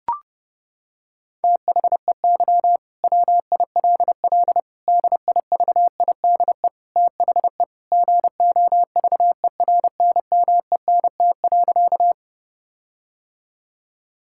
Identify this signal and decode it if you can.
Morse code, heard as a keyed tone, about 30 wpm, 700 Hz. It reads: THEY WILL DIVIDE THE GOVERNMENT.
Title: Sentences from Top 1600 Words 30wpm